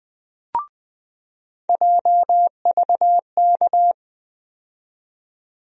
Morse code, heard as a keyed tone, about 20 wpm, 700 Hz. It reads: JVK
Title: USA Call Sign Postfixes - 3 Letters 20wpm